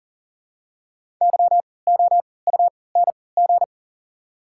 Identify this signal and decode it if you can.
Morse code, heard as a keyed tone, about 40 wpm, spaced as for 23 wpm, 700 Hz. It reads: YOUNG
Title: 3x Character Spacing - Top 500 Words - 40wpm